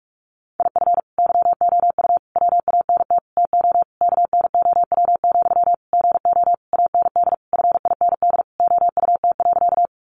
Morse code, heard as a keyed tone, about 44 wpm, 700 Hz. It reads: IF YOU WANT TO KNOW, GO AND FIND OUT.
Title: Sentences from Top 200 Words 45wpm